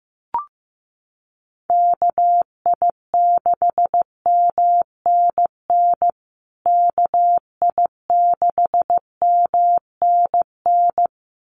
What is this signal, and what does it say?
Morse code, heard as a keyed tone, about 15 wpm, 700 Hz. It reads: KI6MNN KI6MNN
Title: USA Call Signs Repeated 2x 15wpm